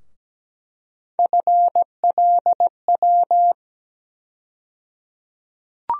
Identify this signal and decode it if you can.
Morse code, heard as a keyed tone, about 17 wpm, 700 Hz. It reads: FLW
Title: USA Call Sign Postfixes - 3 Letters 17wpm